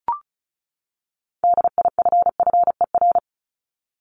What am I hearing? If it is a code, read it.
Morse code, heard as a keyed tone, about 34 wpm, 700 Hz. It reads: DIFFER
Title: Top 500 Words 35wpm